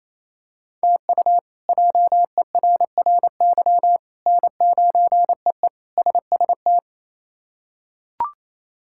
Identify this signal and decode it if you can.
Morse code, heard as a keyed tone, about 28 wpm, 700 Hz. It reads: TU JERRY N9EE SST
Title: K1USN SST QSO Practice - 28wpm